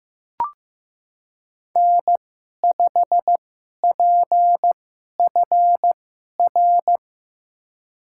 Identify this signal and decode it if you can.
Morse code, heard as a keyed tone, about 15 wpm, spaced as for 11 wpm, 700 Hz. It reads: N5PFR
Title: USA Call Signs - 2x Character Spacing - 15wpm